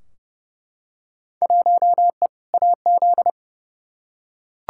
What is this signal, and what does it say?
Morse code, heard as a keyed tone, about 32 wpm, 700 Hz. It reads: 1E AZ
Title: Field Day Exchanges - 30wpm